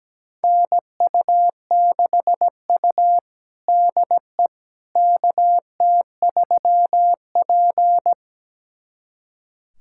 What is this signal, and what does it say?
Morse code, heard as a keyed tone, about 17 wpm, 700 Hz. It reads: NU6U DE KT3P
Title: USA Callsign DE Callsign - 4 Characters - 17wpm